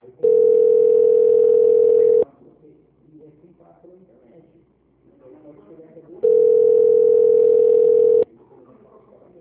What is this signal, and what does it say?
Ringback tone (North America), 440 + 480 Hz, with speech.
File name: out- -2006- - - .78